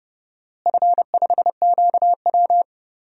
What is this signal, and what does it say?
Morse code, heard as a keyed tone, about 30 wpm, 700 Hz. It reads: F5QW
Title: DXCC-100 Call Signs - 4 Characters - Repeated 3x - 30wpm